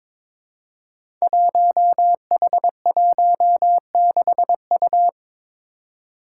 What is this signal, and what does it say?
Morse code, heard as a keyed tone, about 21 wpm, 700 Hz. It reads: 1H16U
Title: Five Challenging Letter-Number Combinations 22wpm